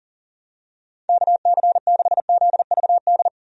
Morse code, heard as a keyed tone, about 39 wpm, 700 Hz. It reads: KC6ZVD
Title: USA Call Signs - 6 Characters - 40wpm